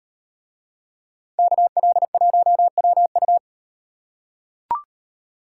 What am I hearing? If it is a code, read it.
Morse code, heard as a keyed tone, about 38 wpm, 700 Hz. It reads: KL1WU